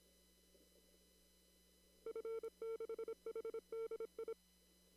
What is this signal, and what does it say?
Morse code, heard as a keyed tone, about 29 wpm, 450 Hz. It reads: F6HDI
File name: CW / kt5j wpx cw-op-k5pi